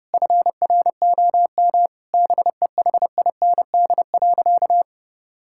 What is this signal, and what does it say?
Morse code, heard as a keyed tone, about 32 wpm, 700 Hz. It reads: FROM BEHIND.
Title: Sentences from Top 1600 Words 30wpm